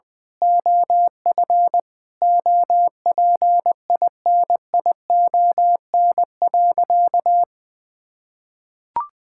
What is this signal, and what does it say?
Morse code, heard as a keyed tone, about 19 wpm, 700 Hz. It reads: OF OPINION.
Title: Sentences from Top 1600 Words 20wpm